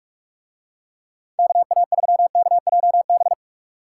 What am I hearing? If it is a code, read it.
Morse code, heard as a keyed tone, about 44 wpm, 690 Hz. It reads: KA3KJB